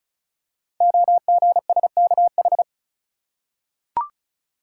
Morse code, heard as a keyed tone, about 36 wpm, 700 Hz. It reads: OGSKH
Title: Code Groups - 5 Letters - 35wpm